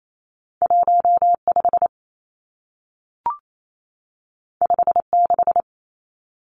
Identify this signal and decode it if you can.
Morse code, heard as a keyed tone, about 28 wpm, 700 Hz. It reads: 15 56
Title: Numbers - 2-Digit 28wpm